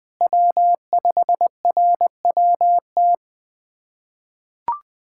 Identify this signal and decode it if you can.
Morse code, heard as a keyed tone, about 20 wpm, 700 Hz. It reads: W5RWT
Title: SOTA QSO Practice 20wpm